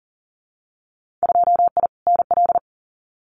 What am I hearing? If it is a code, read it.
Morse code, heard as a keyed tone, about 40 wpm, 700 Hz. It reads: 2I NL
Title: Winter Field Day Exchanges Encore 40wpm